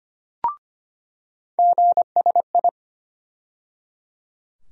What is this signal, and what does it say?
Morse code, heard as a keyed tone, about 25 wpm, 700 Hz. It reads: GSI
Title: USA Call Sign Postfixes - 3 Letters 25wpm